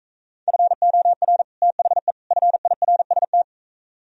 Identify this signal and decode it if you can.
Morse code, heard as a keyed tone, about 44 wpm, 700 Hz. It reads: FOR THE FIRST